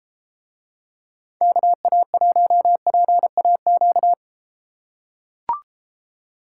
Morse code, heard as a keyed tone, about 32 wpm, 700 Hz. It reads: KA1PAQ